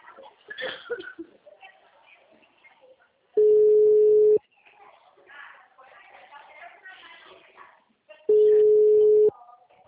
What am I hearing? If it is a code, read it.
Ringback tone (Europe, 425 Hz), with speech.